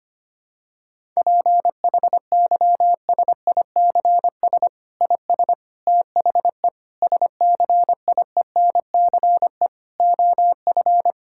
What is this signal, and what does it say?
Morse code, heard as a keyed tone, about 25 wpm, 700 Hz. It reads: PHYSICS IS THE SCIENCE OF
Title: Sentences from Top 1000 Words - Review Part 5 - 25wpm